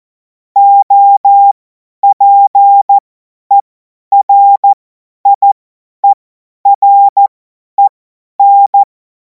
Morse code, heard as a keyed tone, about 14 wpm, spaced as for 10 wpm, 800 Hz. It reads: OPERIEREN
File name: Morsekurs / Witze / Witz 04. August 2026 50 F70